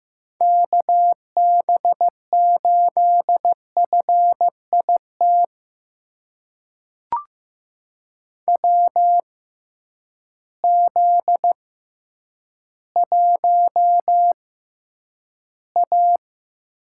Morse code, heard as a keyed tone, about 15 wpm, spaced as for 9 wpm, 700 Hz. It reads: KB8FIT WZ1A
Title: USA Call Signs - 6x Character Spacing - 15wpm